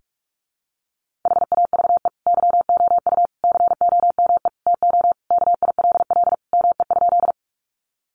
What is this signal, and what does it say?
Morse code, heard as a keyed tone, about 44 wpm, 700 Hz. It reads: HAVE YOU COME TO KILL ME?
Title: Sentences from Top 1400 Words 45wpm